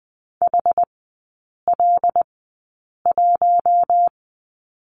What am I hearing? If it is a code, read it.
Morse code, heard as a keyed tone, about 20 wpm, spaced as for 8 wpm, 700 Hz. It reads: HL1
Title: 074 Lesson 13e - Letter L - Three Characters 20wpm